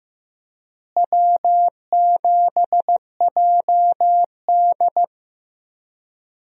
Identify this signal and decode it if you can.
Morse code, heard as a keyed tone, about 15 wpm, 700 Hz. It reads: W7JD
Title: USA Call Signs - 4 Characters - Repeated 3x - 15wpm